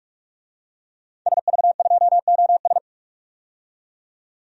Morse code, heard as a keyed tone, about 41 wpm, 700 Hz. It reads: SV2OS
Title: DXCC-100 Call Signs - 5 Characters - 45wpm